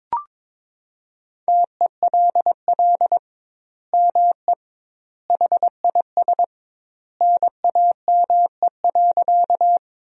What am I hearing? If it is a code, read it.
Morse code, heard as a keyed tone, about 22 wpm, 700 Hz. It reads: TELL ME HIS NAME.
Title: 2x Word-Spacing - Sentences from Top 200 Words - 22wpm